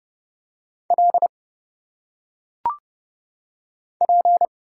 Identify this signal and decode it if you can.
Morse code, heard as a keyed tone, about 30 wpm, 700 Hz. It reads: L P
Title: Single Letters - Phonetic Pronunciation 30wpm